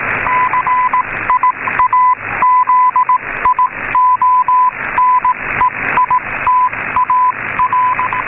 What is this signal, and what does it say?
Morse code, heard as a keyed tone, about 18 wpm, spaced as for 16 wpm, 1000 Hz, with radio noise. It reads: CIAZIONEITAR
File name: mw / 1958 10000 italian time stn